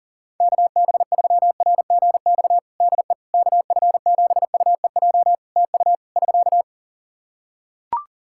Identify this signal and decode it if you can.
Morse code, heard as a keyed tone, about 41 wpm, 700 Hz. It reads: KB3RGX DE KF7UEJ TU <SK>
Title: Practice QSO - Exchange 4 - Ending - 40wpm